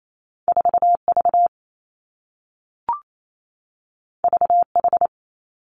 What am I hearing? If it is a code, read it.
Morse code, heard as a keyed tone, about 28 wpm, 700 Hz. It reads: 4 V V H
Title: Two Challenging Letter-Number Combinations 28wpm